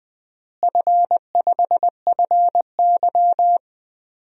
Morse code, heard as a keyed tone, about 21 wpm, 700 Hz. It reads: F5FY
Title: DXCC-100 Call Signs - 4 Characters - Repeated 3x - 20wpm